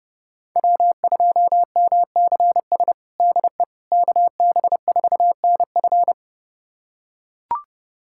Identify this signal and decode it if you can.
Morse code, heard as a keyed tone, about 30 wpm, 700 Hz. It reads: W2MCS DE KB4NF
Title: USA Callsign DE Callsign - 5 Characters - 30wpm